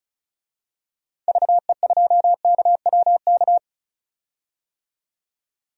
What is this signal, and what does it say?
Morse code, heard as a keyed tone, about 36 wpm, 700 Hz. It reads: VE2KWK
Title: DXCC-100 Call Signs - 6 Characters - 35wpm